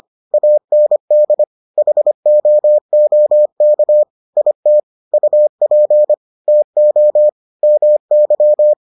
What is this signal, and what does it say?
Morse code, heard as a keyed tone, about 25 wpm, 600 Hz. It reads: AND HOOK IT UP TO MY